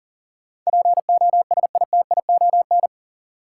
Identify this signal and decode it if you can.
Morse code, heard as a keyed tone, about 39 wpm, 700 Hz. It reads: POSITION
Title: Sentences from Top 900 Words - New Word Drill - 40wpm